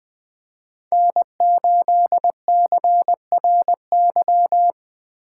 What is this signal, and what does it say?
Morse code, heard as a keyed tone, about 21 wpm, 700 Hz. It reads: N8CRY
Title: USA Call Signs - 5 Characters - 6x Character Spacing - 20wpm